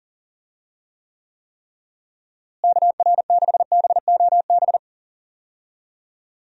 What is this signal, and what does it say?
Morse code, heard as a keyed tone, about 40 wpm, 700 Hz. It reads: KR6BOB
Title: USA Call Signs - 6 Characters - Repeated 3x - 40wpm